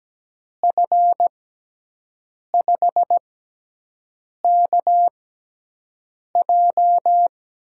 Morse code, heard as a keyed tone, about 17 wpm, spaced as for 6 wpm, 700 Hz. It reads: F5KJ
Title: USA Call Signs - 5 Characters - 6x Character Spacing - 17wpm